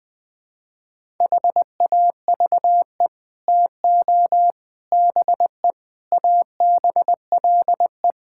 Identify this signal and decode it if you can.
Morse code, heard as a keyed tone, about 20 wpm, 700 Hz. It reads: HAVE TO BE ABLE
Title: Build A Phrase - 5-Word Phrases - 20wpm